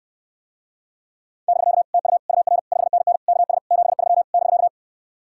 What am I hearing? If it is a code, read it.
Morse code, heard as a keyed tone, about 69 wpm, 700 Hz. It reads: <BK> TU UR 5NN KS 73 <BK>